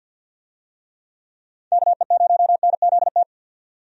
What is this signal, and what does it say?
Morse code, heard as a keyed tone, about 51 wpm, 700 Hz. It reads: KE0NZT
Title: USA Call Signs - 6 Characters - 50wpm